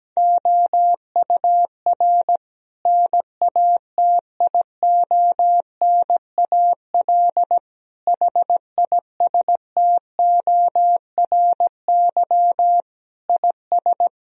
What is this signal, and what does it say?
Morse code, heard as a keyed tone, about 17 wpm, 700 Hz. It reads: OUR NATIONAL HISTORY IS
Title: Sentences from Top 1000 Words - Review Part 4 - 17wpm